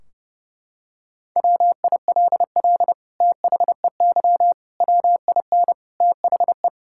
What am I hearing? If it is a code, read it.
Morse code, heard as a keyed tone, about 30 wpm, 700 Hz. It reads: WILL THEY WIN THE
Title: Sentences from Top 1600 Words 30wpm